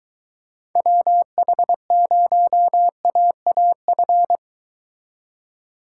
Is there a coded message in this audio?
Yes: Morse code from 0.8 to 4.4 s, heard as a keyed tone, about 23 wpm, 700 Hz. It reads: WH0AAF